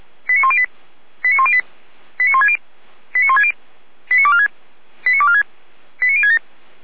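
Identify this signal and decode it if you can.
Five-tone selective call (ZVEI), heard as quick runs of five tones, tones of about 70 ms. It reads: 89108 89108 89160 89160 89246 89246 89067
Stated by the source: ZVEI five-tone selective call received on SDR.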